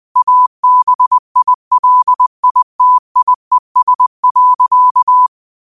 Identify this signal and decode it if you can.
Morse code, heard as a keyed tone, about 20 wpm, 1000 Hz. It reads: ABILITIES.